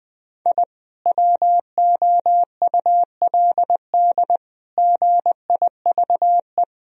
Morse code, heard as a keyed tone, about 20 wpm, 700 Hz. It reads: I WOULD GIVE
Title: Sentences from Top 1000 Words - Review Part 1 - 20wpm